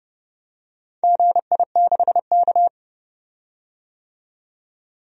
Morse code, heard as a keyed tone, about 30 wpm, 700 Hz. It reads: GI6K
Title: DXCC-100 Call Signs - 4 Characters - 30wpm